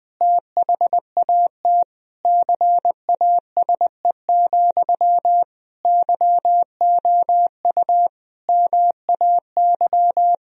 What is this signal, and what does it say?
Morse code, heard as a keyed tone, about 20 wpm, 700 Hz. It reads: THAT CASE, YOU MAY
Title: Sentences from Top 1000 Words - Review Part 1 - 20wpm